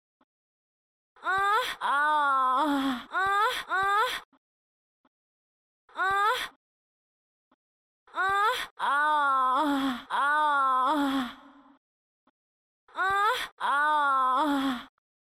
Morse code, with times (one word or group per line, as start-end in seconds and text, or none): none